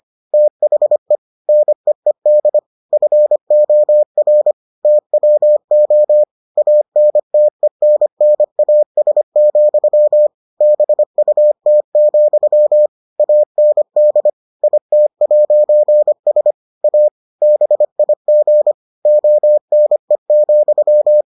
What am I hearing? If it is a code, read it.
Morse code, heard as a keyed tone, about 25 wpm, 600 Hz. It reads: THE NEED FOR TWO ANTENNAS, BUT, AND IT'S A BIG ONE,